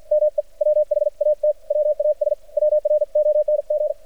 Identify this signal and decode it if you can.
Morse code, heard as a keyed tone, about 49 wpm, 600 Hz. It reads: ME WHAT WAS WRONG
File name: archived / CRSnet